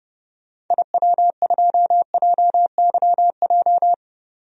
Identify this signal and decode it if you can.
Morse code, heard as a keyed tone, about 32 wpm, 700 Hz. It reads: IW2JYJ
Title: DXCC-100 Call Signs - 6 Characters - 30wpm